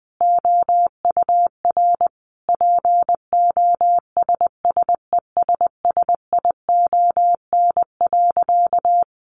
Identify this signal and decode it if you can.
Morse code, heard as a keyed tone, about 20 wpm, 700 Hz. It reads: OUR POSSESSION.
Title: Sentences from Top 1400 Words 20wpm